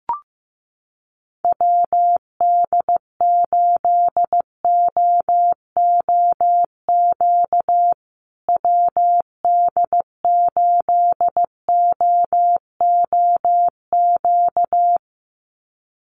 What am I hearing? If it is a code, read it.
Morse code, heard as a keyed tone, about 15 wpm, 700 Hz. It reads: WD8OOQ WD8OOQ
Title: USA Call Signs Repeated 2x 15wpm